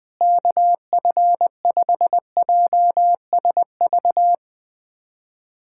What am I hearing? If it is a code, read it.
Morse code, heard as a keyed tone, about 20 wpm, 700 Hz. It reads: KF5JSV
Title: USA Callsign DE Callsign - 6 Characters - 20wpm